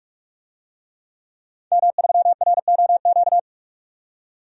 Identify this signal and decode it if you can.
Morse code, heard as a keyed tone, about 43 wpm, 700 Hz. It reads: M3ROQ
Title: DXCC-100 Call Signs - 5 Characters - 45wpm